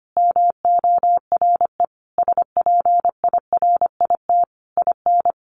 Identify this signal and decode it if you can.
Morse code, heard as a keyed tone, about 25 wpm, 700 Hz. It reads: MORE SPIRIT IN
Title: Sentences from Top 1400 Words 25wpm